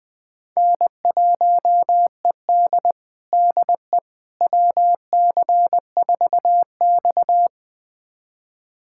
Morse code, heard as a keyed tone, about 20 wpm, 700 Hz. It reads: N1ED DE WC4X
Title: USA Callsign DE Callsign - 4 Characters - 20wpm